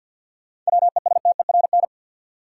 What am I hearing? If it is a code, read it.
Morse code, heard as a keyed tone, about 47 wpm, 700 Hz. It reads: WESTERN
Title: Sentences from Top 1000 Words - Review Part 7 - New Word Drill - 50wpm